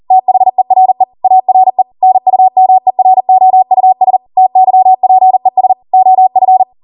Morse code, heard as a keyed tone, about 40 wpm, 750 Hz. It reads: THERE ARE NUMEROUS TYPES OF